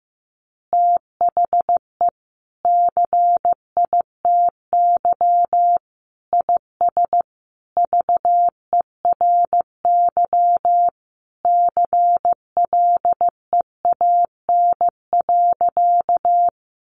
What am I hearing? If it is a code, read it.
Morse code, heard as a keyed tone, about 15 wpm, 700 Hz. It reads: THE CITY IS VERY CLEAN.
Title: Easy Phrases 15wpm